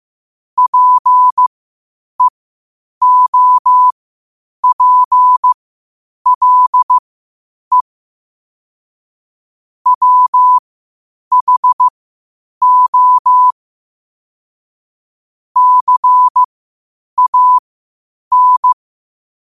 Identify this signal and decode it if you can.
Morse code, heard as a keyed tone, about 15 wpm, spaced as for 8 wpm, 1000 Hz. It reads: PEOPLE WHO CAN
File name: qotdfiles / 8wpm